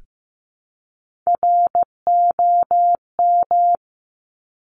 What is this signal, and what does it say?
Morse code, heard as a keyed tone, about 15 wpm, 700 Hz. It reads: ROM
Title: Top Three Letter Combinations in Words 15wpm